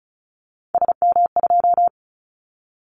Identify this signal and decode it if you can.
Morse code, heard as a keyed tone, about 36 wpm, 700 Hz. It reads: SM2
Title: Callsign Prefixes for Top 25 DXCC - 35wpm